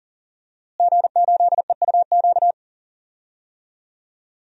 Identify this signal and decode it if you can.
Morse code, heard as a keyed tone, about 40 wpm, 700 Hz. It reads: G8EUQ
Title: DXCC-100 Call Signs - 5 Characters - Repeated 3x - 40wpm